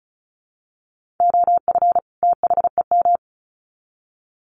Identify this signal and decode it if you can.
Morse code, heard as a keyed tone, about 34 wpm, 700 Hz. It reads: OF THEM
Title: Sets of 2 Words 35wpm